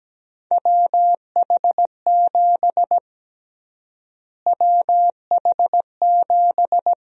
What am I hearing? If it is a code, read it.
Morse code, heard as a keyed tone, about 17 wpm, 700 Hz. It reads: WH7 WH7
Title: USA Call Sign Prefixes - Repeated 2x - 17wpm